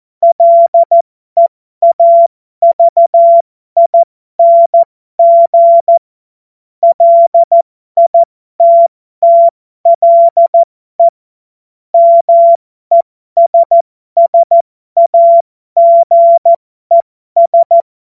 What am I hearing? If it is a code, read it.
Morse code, heard as a keyed tone, about 14 wpm, spaced as for 12 wpm, 680 Hz. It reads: LEAVING LITTLE MESSAGES
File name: Morsekurs / Witze